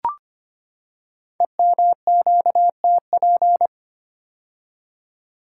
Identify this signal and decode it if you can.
Morse code, heard as a keyed tone, about 25 wpm, 700 Hz. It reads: EMQTP
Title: Code Groups - 5 Letters - 25wpm